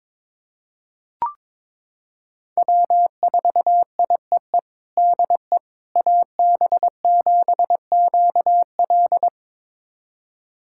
Morse code, heard as a keyed tone, about 21 wpm, 700 Hz. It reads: W4IEE DE AB7QL
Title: USA Callsign DE Callsign - 5 Characters - 22wpm